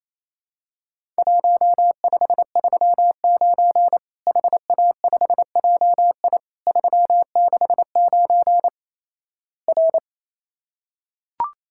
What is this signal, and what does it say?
Morse code, heard as a keyed tone, about 28 wpm, 700 Hz. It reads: 1539 HA5JI 369 R
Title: Worked All Europe DX Contest - QTC Exchange - 28wpm